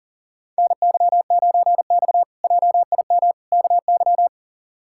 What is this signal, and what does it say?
Morse code, heard as a keyed tone, about 37 wpm, 700 Hz. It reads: NY9X JIM KY
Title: K1USN SST QSO Practice - 40wpm